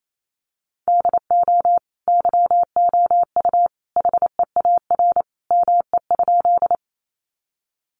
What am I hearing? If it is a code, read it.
Morse code, heard as a keyed tone, about 28 wpm, 700 Hz. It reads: DO YOU HEAR ME?
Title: Sentences from Top 300 Words 28wpm